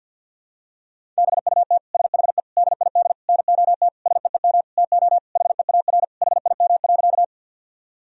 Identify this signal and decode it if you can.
Morse code, heard as a keyed tone, about 53 wpm, 700 Hz. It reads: BUT SHE DID NOT SEEM TO HEAR HIM.
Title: Sentences from Top 1000 Words - Review Part 1 - 50wpm